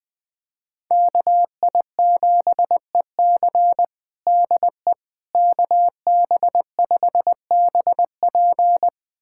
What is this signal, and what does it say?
Morse code, heard as a keyed tone, about 20 wpm, 700 Hz. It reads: KI7EC DE KB5BP
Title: USA Callsign DE Callsign - 5 Characters - 20wpm